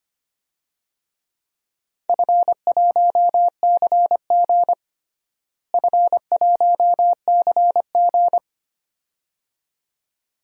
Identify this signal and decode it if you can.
Morse code, heard as a keyed tone, about 25 wpm, 700 Hz. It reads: F1CG F1CG
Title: DXCC-100 Call Signs - 4 Characters - Repeated 2x - 25wpm